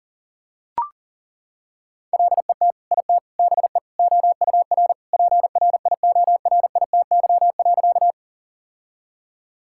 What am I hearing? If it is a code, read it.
Morse code, heard as a keyed tone, about 37 wpm, 700 Hz. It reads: LET IT BE OUR PRIORITY.
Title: Sentences from Top 1600 Words 40wpm